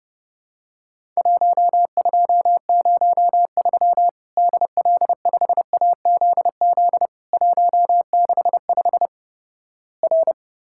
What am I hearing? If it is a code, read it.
Morse code, heard as a keyed tone, about 30 wpm, 700 Hz. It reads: 1203 DL5AZZ 165 R
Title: Worked All Europe DX Contest - QTC Exchange - 30wpm